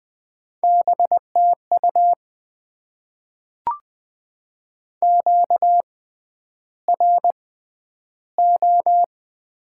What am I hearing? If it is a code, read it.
Morse code, heard as a keyed tone, about 20 wpm, spaced as for 10 wpm, 700 Hz. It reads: BTU QRO
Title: Top 90 QSO Elements - 6x Character Spacing - 20wpm